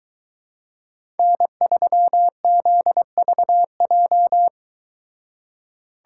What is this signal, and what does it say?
Morse code, heard as a keyed tone, about 22 wpm, 700 Hz. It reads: N3ZVJ